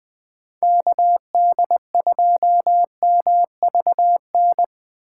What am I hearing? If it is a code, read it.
Morse code, heard as a keyed tone, about 20 wpm, 700 Hz. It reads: KD2MVN
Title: Speed Racing - USA 2x3 Call Signs 17wpm